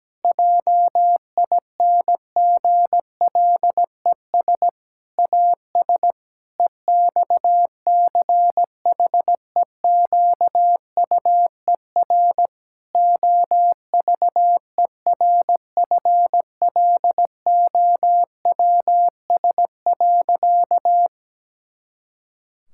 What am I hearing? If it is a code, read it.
Morse code, heard as a keyed tone, about 17 wpm, 700 Hz. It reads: JINGLES AS EXCHEQUER OVERFLOWS.
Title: Pangram Challenge v2 - 17wpm